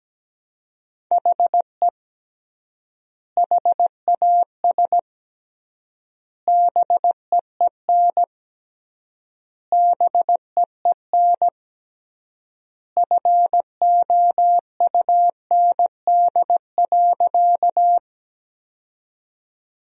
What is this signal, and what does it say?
Morse code, heard as a keyed tone, about 17 wpm, 700 Hz. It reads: HE HAS BEEN BEEN FOUND.
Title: 3x Word-Spacing - Sentences from Top 300 Words - 17wpm